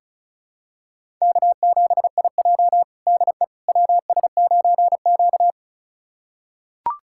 Morse code, heard as a keyed tone, about 36 wpm, 700 Hz. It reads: K7IJ DE WS9Q
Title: USA Callsign DE Callsign - 4 Characters - 35wpm